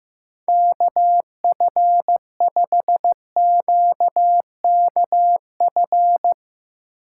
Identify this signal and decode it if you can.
Morse code, heard as a keyed tone, about 15 wpm, 700 Hz. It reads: KF5QKF
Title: Speed Racing - USA 2x3 Call Signs 15wpm